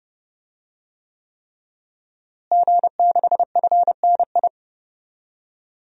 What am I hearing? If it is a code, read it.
Morse code, heard as a keyed tone, about 32 wpm, 700 Hz. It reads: G6FNI
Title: DXCC-100 Call Signs - 5 Characters - Repeated 3x - 30wpm